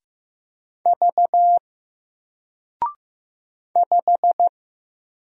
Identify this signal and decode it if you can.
Morse code, heard as a keyed tone, about 15 wpm, 700 Hz. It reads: V 5
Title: Single Letter-Number Alt 15wpm